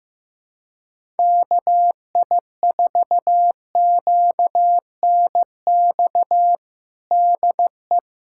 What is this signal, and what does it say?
Morse code, heard as a keyed tone, about 16 wpm, 700 Hz. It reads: KI4QNX DE
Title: USA Callsign DE Callsign - 6 Characters - 15wpm